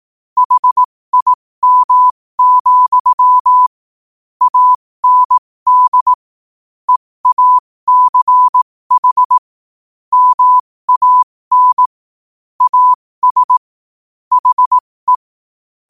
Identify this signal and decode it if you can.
Morse code, heard as a keyed tone, about 18 wpm, spaced as for 15 wpm, 1000 Hz. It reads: HIM, AND EACH MAN AS HE
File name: qotdfiles / 15wpm